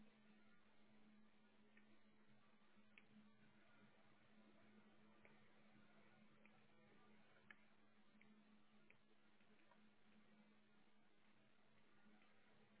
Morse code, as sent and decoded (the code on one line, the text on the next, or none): none